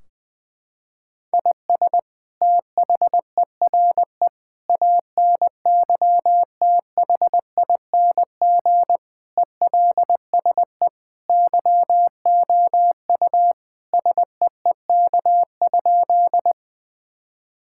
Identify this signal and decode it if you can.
Morse code, heard as a keyed tone, about 21 wpm, 700 Hz. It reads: IS THERE ANYTHING ELSE YOU SEEK?
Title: Sentences from Top 1600 Words 20wpm